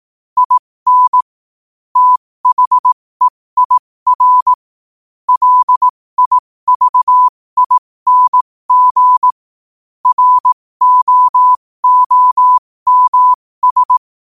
Morse code, heard as a keyed tone, about 18 wpm, spaced as for 15 wpm, 1000 Hz. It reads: IN THEIR LIVING ROOMS